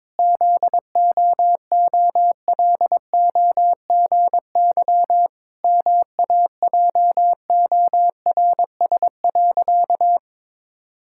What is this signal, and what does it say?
Morse code, heard as a keyed tone, about 22 wpm, 700 Hz. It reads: ZOOLOGY MAJORS.
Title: Pangram Challenge v2 - 22wpm